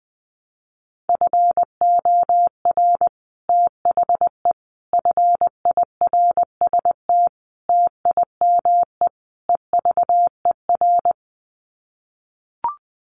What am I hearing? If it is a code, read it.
Morse code, heard as a keyed tone, about 20 wpm, 700 Hz. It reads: FOR THE FIRST TIME EVER
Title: Sets of 5 Words Encore2 20wpm